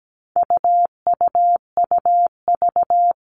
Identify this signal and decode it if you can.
Morse code, heard as a keyed tone, about 17 wpm, 700 Hz. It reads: UUUV
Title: All Cut Numbers - 4-Digit 17wpm